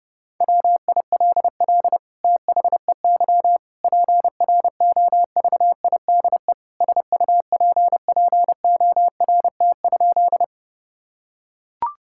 Morse code, heard as a keyed tone, about 30 wpm, 700 Hz. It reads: WILL THEY PROVIDE SUPPORT?
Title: Sentences from Top 1000 Words - Review Part 5 - 30wpm